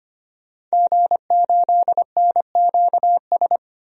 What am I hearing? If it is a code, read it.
Morse code, heard as a keyed tone, about 25 wpm, 700 Hz. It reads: G8NQS
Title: DXCC-100 Call Signs - 5 Characters - 25wpm